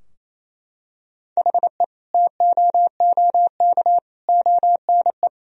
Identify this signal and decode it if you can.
Morse code, heard as a keyed tone, about 28 wpm, 700 Hz. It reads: HE TOOK ONE
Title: Sentences from Top 1000 Words - Review Part 2 - 28wpm